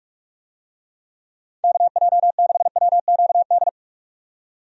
Morse code, heard as a keyed tone, about 42 wpm, 700 Hz. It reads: KJ6WQD
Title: USA Call Signs - 6 Characters - 45wpm